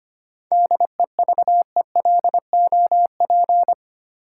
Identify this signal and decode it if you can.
Morse code, heard as a keyed tone, about 25 wpm, 700 Hz. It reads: DEVELOP
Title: Sentences from Top 1000 Words - Review Part 2 - New Word Drill - 25wpm